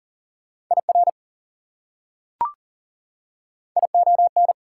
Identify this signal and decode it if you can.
Morse code, heard as a keyed tone, about 40 wpm, 700 Hz. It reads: IR ION
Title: Prefixes and Suffixes 40wpm